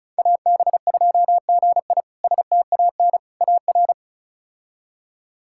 Morse code, heard as a keyed tone, about 36 wpm, 700 Hz. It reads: AB2GI STAN AR
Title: K1USN SST QSO Practice - 35wpm